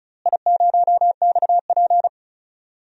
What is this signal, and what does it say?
Morse code, heard as a keyed tone, about 36 wpm, 690 Hz. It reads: I0XP
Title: DXCC-100 Call Signs - 4 Characters - Repeated 3x - 35wpm